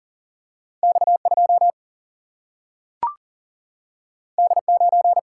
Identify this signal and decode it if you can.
Morse code, heard as a keyed tone, about 40 wpm, 700 Hz. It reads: X2 D9
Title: Single-Letter Single-Number 40wpm